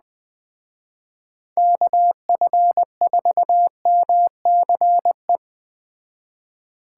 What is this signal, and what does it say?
Morse code, heard as a keyed tone, about 20 wpm, 700 Hz. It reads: KF4MCE
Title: Speed Racing - USA 2x3 Call Signs 17wpm